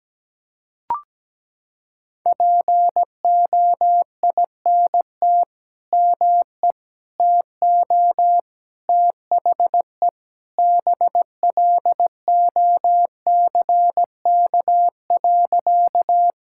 Morse code, heard as a keyed tone, about 17 wpm, 700 Hz. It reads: POINT ME TO THE BLOCK.
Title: Sentences from Top 1600 Words 17wpm